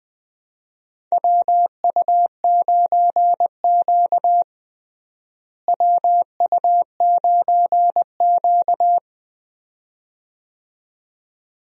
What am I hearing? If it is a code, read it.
Morse code, heard as a keyed tone, about 20 wpm, 700 Hz. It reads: WU9Q WU9Q
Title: USA Call Signs - 4 Characters - Repeated 2x - 20wpm